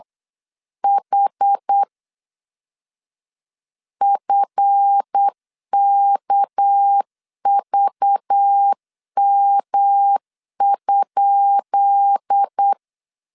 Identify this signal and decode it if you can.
Morse code, heard as a keyed tone, about 8 wpm, 800 Hz. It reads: H FKVM?
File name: LEZIONI / 10 SIGNS / 10B INT 8wpm 3534